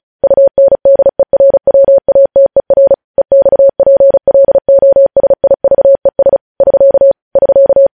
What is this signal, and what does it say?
Morse code, heard as a keyed tone, about 36 wpm, 550 Hz. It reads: UNDERWATER EXPLOSIVES <SK> <SK>
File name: podcast / News headlines-1x-35x35